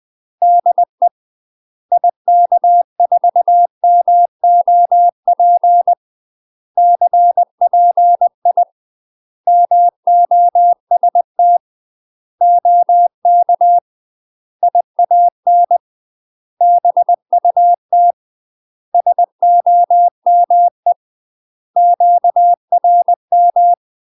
Morse code, heard as a keyed tone, about 20 wpm, 700 Hz. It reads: DE IK4MOP CPI MOST OK IAN BUT SOME QRM
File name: UK MASTER QSOs TRAINING / 20WPM / UK MASTER QSOs 0071